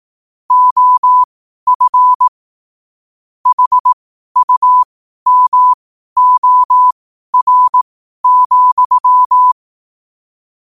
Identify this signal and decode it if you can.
Morse code, heard as a keyed tone, about 18 wpm, spaced as for 12 wpm, 1000 Hz. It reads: OF HUMOR,
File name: qotdfiles / 12wpm